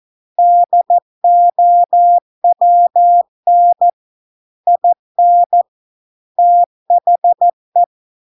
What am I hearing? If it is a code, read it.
Morse code, heard as a keyed tone, about 14 wpm, 700 Hz. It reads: DOWN IN THE